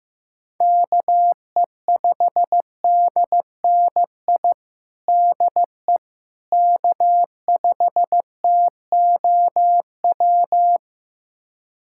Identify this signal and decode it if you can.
Morse code, heard as a keyed tone, about 15 wpm, 700 Hz. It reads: KE5DNI DE K5TOW
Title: USA Callsign DE Callsign 15wpm